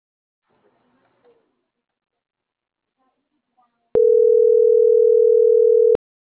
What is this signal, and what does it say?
Ringback tone (North America), 440 + 480 Hz.